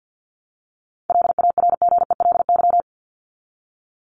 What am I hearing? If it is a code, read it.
Morse code, heard as a keyed tone, about 50 wpm, 700 Hz. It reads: LARGELY
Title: LY Suffix - Top 100 Words with Base Words 50wpm